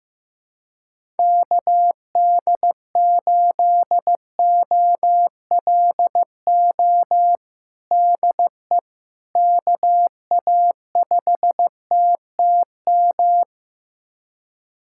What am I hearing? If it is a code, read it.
Morse code, heard as a keyed tone, about 15 wpm, 700 Hz. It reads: KD8OLO DE KA5TTM
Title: USA Callsign DE Callsign 15wpm